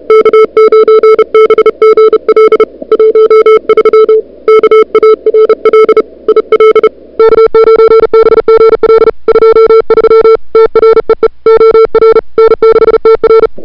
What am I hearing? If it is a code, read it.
Morse code, heard as a keyed tone, about 34 wpm, 440 Hz. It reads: K9BGL 13 KARL IL K9BGL 23 TREE OR N6TR
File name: CWSprint / February 2007 CW Sprint / 01-2007 CW Sprint Hour1